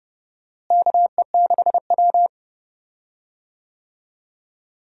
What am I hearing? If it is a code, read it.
Morse code, heard as a keyed tone, about 30 wpm, 700 Hz. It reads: KE6W
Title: USA Call Signs - 4 Characters - Repeated 2x - 30wpm